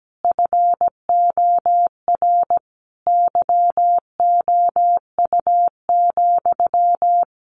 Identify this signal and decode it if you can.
Morse code, heard as a keyed tone, about 17 wpm, 700 Hz. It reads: FOR YOU,
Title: A Sound Of Thunder 17wpm